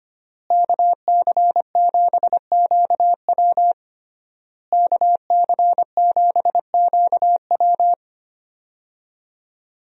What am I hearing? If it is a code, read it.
Morse code, heard as a keyed tone, about 25 wpm, 700 Hz. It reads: KC7QW KC7QW
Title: USA Call Signs - 5 Characters - Repeated 3x - 25wpm